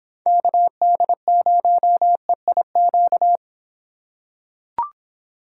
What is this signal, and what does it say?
Morse code, heard as a keyed tone, about 25 wpm, 700 Hz. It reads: KD0EIQ